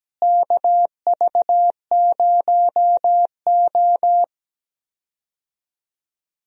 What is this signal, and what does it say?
Morse code, heard as a keyed tone, about 17 wpm, 700 Hz. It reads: KV0O
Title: USA Call Signs - 4 Characters - Repeated 2x - 17wpm